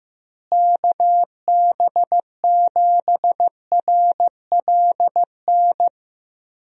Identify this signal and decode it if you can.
Morse code, heard as a keyed tone, about 16 wpm, 700 Hz. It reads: KB7RLN
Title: USA Call Signs - 6 Characters - Repeated 2x - 15wpm